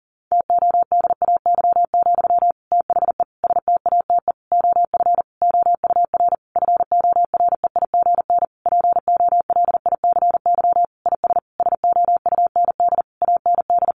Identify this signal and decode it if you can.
Morse code, heard as a keyed tone, about 37 wpm, 700 Hz. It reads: TODAY, THE STATE OF OUR FOREIGN POLICY IS SOUND AND
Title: Sentences from Top 1400 Words 40wpm